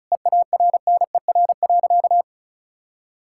Morse code, heard as a keyed tone, about 34 wpm, 700 Hz. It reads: EARNER.
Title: Pangram Challenge v2 - 35wpm